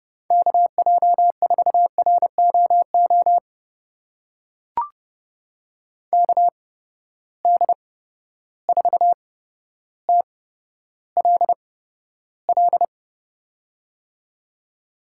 Morse code, heard as a keyed tone, about 30 wpm, spaced as for 13 wpm, 700 Hz. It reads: KJ4ROO KD4TLL
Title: USA Call Signs - 6 Characters - 8x Character Spacing - 30wpm